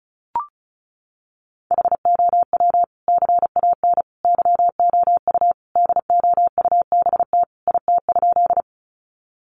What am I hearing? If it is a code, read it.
Morse code, heard as a keyed tone, about 34 wpm, 700 Hz. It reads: HOW CAN YOU DOUBT IT?
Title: Sentences from Top 1500 Words 35wpm